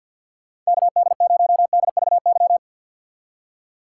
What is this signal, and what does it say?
Morse code, heard as a keyed tone, about 65 wpm, spaced as for 57 wpm, 700 Hz. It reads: KD0DVY
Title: Speed Racing - USA 2x3 Call Signs 50wpm